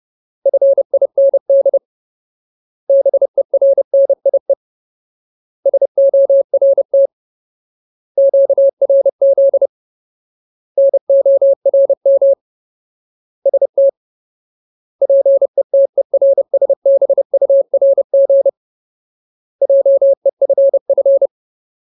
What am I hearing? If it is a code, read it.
Morse code, heard as a keyed tone, about 30 wpm, 550 Hz. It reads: FIND BERNIE SORT QRZ NORM ST PETERSBURG JEFF
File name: Ultimate Word List / 30WPM Ultimate